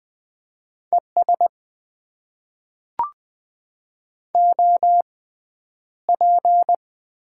Morse code, heard as a keyed tone, about 20 wpm, spaced as for 10 wpm, 700 Hz. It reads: ES OP
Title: Top 90 QSO Elements - 6x Character Spacing - 20wpm